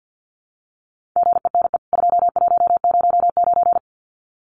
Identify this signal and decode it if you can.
Morse code, heard as a keyed tone, about 50 wpm, 700 Hz. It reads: GENE 2109
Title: CWT Exchange 50wpm